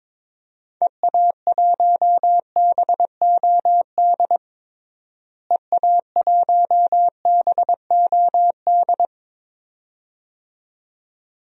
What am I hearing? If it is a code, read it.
Morse code, heard as a keyed tone, about 22 wpm, 700 Hz. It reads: EA1BOD EA1BOD
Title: DXCC-100 Call Signs - 6 Characters - Repeated 2x - 22wpm